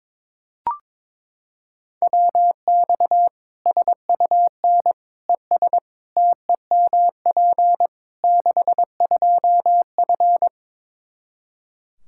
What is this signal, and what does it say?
Morse code, heard as a keyed tone, about 21 wpm, 700 Hz. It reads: WX SUN ES TEMP 62F
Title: Practice QSO - Exchange 2 - Info - 22wpm